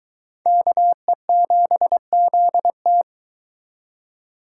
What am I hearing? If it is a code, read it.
Morse code, heard as a keyed tone, about 22 wpm, 700 Hz. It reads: KE7ZT